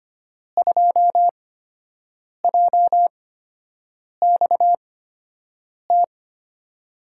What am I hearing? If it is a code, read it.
Morse code, heard as a keyed tone, about 25 wpm, spaced as for 7 wpm, 700 Hz. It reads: 2JXT
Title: USA Call Signs - 6 Characters - 8x Character Spacing - 25wpm